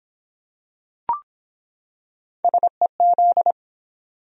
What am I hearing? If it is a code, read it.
Morse code, heard as a keyed tone, about 23 wpm, 700 Hz. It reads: SEZ